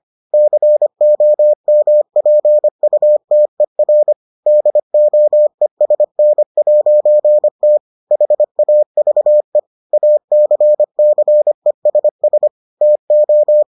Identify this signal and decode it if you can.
Morse code, heard as a keyed tone, about 25 wpm, 600 Hz. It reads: COMPUTER DOESN'T HAVE ACCESS TO